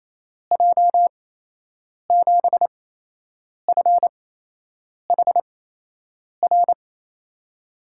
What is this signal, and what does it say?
Morse code, heard as a keyed tone, about 28 wpm, spaced as for 8 wpm, 700 Hz. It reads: J7FHR
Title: USA Call Signs - 6 Characters - 8x Character Spacing - 28wpm